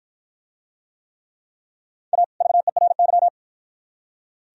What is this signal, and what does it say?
Morse code, heard as a keyed tone, about 51 wpm, 700 Hz. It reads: A VERY